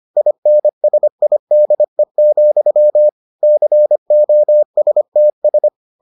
Morse code, heard as a keyed tone, about 25 wpm, 600 Hz. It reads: INSIDE, COSTS